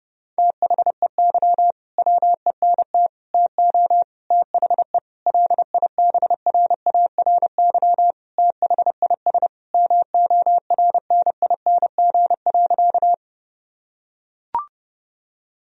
Morse code, heard as a keyed tone, about 30 wpm, 700 Hz. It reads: THEY WENT TO THE LIBRARY THIS MORNING.
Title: Sentences from Top 1000 Words - Review Part 4 - 30wpm